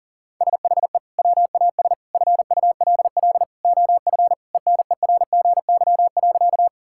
Morse code, heard as a keyed tone, about 40 wpm, 700 Hz. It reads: SHE WAS FULL OF ENERGY.
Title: Sentences from Top 1000 Words - Review Part 6 - 40wpm